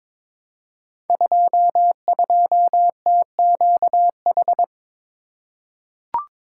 Morse code, heard as a keyed tone, about 22 wpm, 700 Hz. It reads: 22TQH
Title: Code Groups - 5 Letter:Numbers - 22wpm